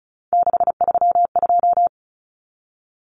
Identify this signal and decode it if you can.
Morse code, heard as a keyed tone, about 34 wpm, 700 Hz. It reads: 632
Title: Numbers - 3-Digit 35wpm